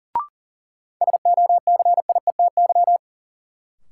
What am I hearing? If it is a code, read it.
Morse code, heard as a keyed tone, about 40 wpm, 700 Hz. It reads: SOCIETY
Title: Sentences from Top 700 Words - New Word Drill - 40wpm